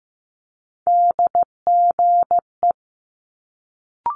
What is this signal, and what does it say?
Morse code, heard as a keyed tone, about 15 wpm, 700 Hz. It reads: DGE
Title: Multi-letter Phonemes Quick Fire 15wpm